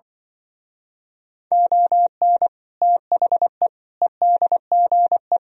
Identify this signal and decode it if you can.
Morse code, heard as a keyed tone, about 27 wpm, 700 Hz. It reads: ON THE EDGE